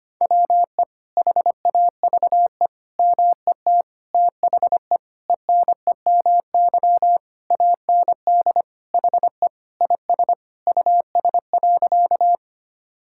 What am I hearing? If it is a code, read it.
Morse code, heard as a keyed tone, about 25 wpm, 700 Hz. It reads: WE HAVE MET THE ENEMY AND HE IS US.
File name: practice / 25wpm / Sentences with 9-Word Phrases - 25wpm